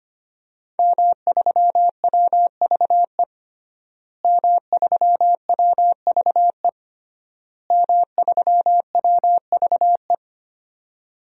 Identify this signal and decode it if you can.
Morse code, heard as a keyed tone, about 25 wpm, 700 Hz. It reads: M3WVE M3WVE M3WVE
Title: DXCC-100 Call Signs - 5 Characters - Repeated 3x - 25wpm